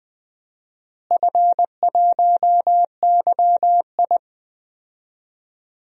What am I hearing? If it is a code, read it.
Morse code, heard as a keyed tone, about 20 wpm, 700 Hz. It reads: F1YI
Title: DXCC-100 Call Signs - 4 Characters - Repeated 3x - 20wpm